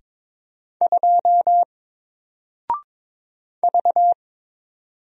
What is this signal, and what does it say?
Morse code, heard as a keyed tone, about 22 wpm, 700 Hz. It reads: 2 V
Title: Single Character (Letter:Number:Punctuation:Prosign) Alt 22wpm